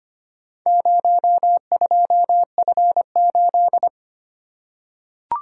Morse code, heard as a keyed tone, about 25 wpm, 690 Hz. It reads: 02F8
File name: practice / 25wpm / Four Letter-Number Combinations - 25wpm